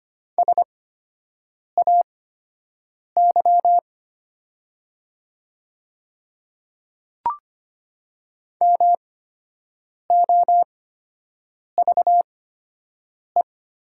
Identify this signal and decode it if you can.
Morse code, heard as a keyed tone, about 25 wpm, spaced as for 7 wpm, 700 Hz. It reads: SAY MOVE
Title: 8x Character Spacing - Top 200 Words - 25wpm